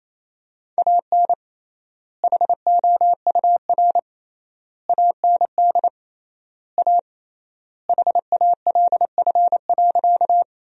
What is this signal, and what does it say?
Morse code, heard as a keyed tone, about 28 wpm, 700 Hz. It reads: AN HOUR AND A HALF.
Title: 3x Word-Spacing - Sentences from Top 400 Words - 28wpm